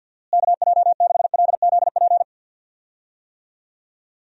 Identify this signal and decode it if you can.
Morse code, heard as a keyed tone, about 51 wpm, 700 Hz. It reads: KJ6LZP
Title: USA Call Signs - 6 Characters - Repeated 2x - 50wpm